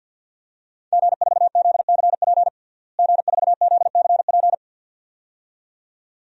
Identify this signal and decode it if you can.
Morse code, heard as a keyed tone, about 52 wpm, 700 Hz. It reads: G4ZCP G4ZCP
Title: DXCC-100 Call Signs - 5 Characters - Repeated 2x - 50wpm